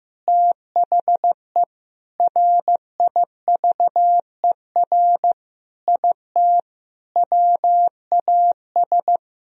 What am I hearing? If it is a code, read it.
Morse code, heard as a keyed tone, about 15 wpm, 700 Hz. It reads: THE RIVER IT WAS
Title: Sentences from Top 1000 Words - Review Part 2 - 15wpm